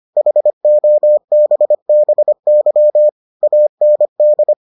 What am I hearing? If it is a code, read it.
Morse code, heard as a keyed tone, about 25 wpm, 600 Hz. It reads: HOBBY AND